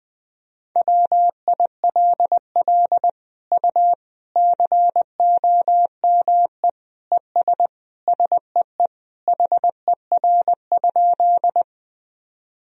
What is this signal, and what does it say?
Morse code, heard as a keyed tone, about 20 wpm, 700 Hz. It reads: WILL U COME ES SEE HER?
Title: Sentences from Top 100 Words and Abbreviations 20wpm